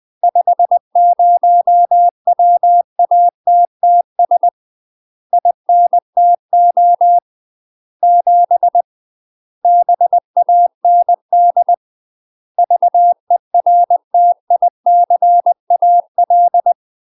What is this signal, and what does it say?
Morse code, heard as a keyed tone, about 20 wpm, 700 Hz. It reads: 50 WATTS INTO 7 BAND VERTICAL
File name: UK MASTER QSOs TRAINING / 20WPM / UK MASTER QSOs 0008